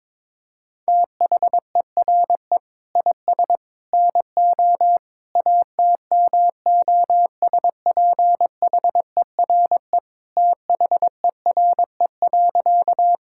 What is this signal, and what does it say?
Morse code, heard as a keyed tone, about 22 wpm, 700 Hz. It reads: THERE IS NO ATMOSPHERE THERE.
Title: Sentences from Top 1600 Words 22wpm